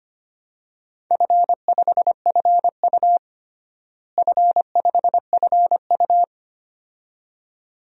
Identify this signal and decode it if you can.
Morse code, heard as a keyed tone, about 25 wpm, 700 Hz. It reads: F5FU F5FU
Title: DXCC-100 Call Signs - 4 Characters - Repeated 2x - 25wpm